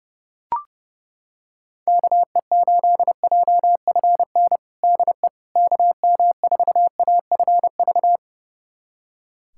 Morse code, heard as a keyed tone, about 30 wpm, 700 Hz. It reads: KE8JFN DE KM4AFV
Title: USA Callsign DE Callsign - 6 Characters - 30wpm